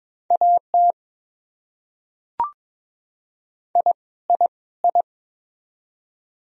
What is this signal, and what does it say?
Morse code, heard as a keyed tone, about 21 wpm, 700 Hz. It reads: AT I I I
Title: Top 100 Words Repeated 22wpm